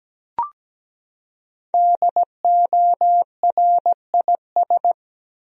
Morse code, heard as a keyed tone, about 17 wpm, 700 Hz. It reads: DORIS
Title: USA Common Names - Max 5 Letters - 17wpm